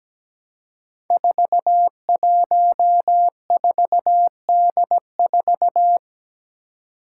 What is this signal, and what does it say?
Morse code, heard as a keyed tone, about 17 wpm, 700 Hz. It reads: 414D4
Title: Five Challenging Letter-Number Combinations 17wpm